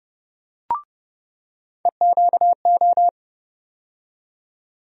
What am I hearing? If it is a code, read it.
Morse code, heard as a keyed tone, about 31 wpm, 700 Hz. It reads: EQO
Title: USA Call Sign Postfixes - 3 Letters 30wpm